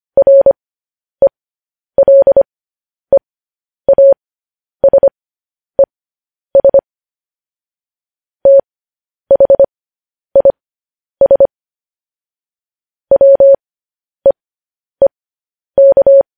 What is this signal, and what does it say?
Morse code, heard as a keyed tone, about 25 wpm, spaced as for 10 wpm, 550 Hz. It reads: RELEASES THIS WEEK
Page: news-headlines-in-morse-code-at-10-wpm